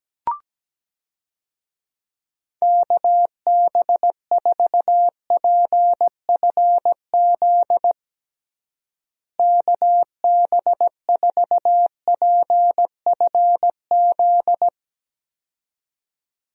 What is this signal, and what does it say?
Morse code, heard as a keyed tone, about 17 wpm, 700 Hz. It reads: KB4PFZ KB4PFZ
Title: USA Call Signs - 6 Characters - Repeated 2x - 17wpm